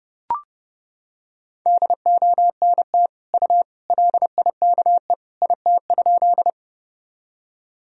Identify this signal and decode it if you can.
Morse code, heard as a keyed tone, about 31 wpm, 700 Hz. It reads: DONT U LIKE IT?
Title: Sentences from Top 300 Words and Abbreviations 30wpm